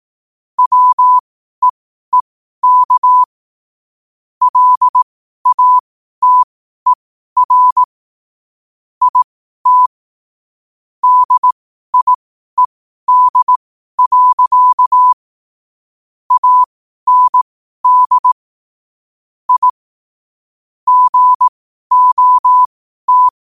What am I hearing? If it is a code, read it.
Morse code, heard as a keyed tone, about 18 wpm, spaced as for 12 wpm, 1000 Hz. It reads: WEEK LATER IT DIED. AND I GOT
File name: qotdfiles / 12wpm